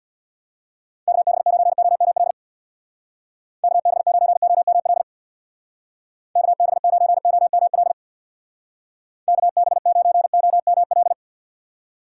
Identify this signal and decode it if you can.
Morse code, heard as a keyed tone, about 64 wpm, 700 Hz. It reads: KB9OML KB9OML KB9OML KB9OML